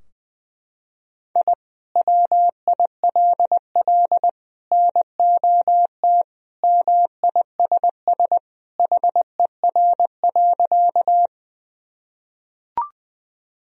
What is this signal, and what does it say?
Morse code, heard as a keyed tone, about 19 wpm, 700 Hz. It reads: I WILL NOT MISS HER.
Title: Sentences from Top 1000 Words - Review Part 3 - 20wpm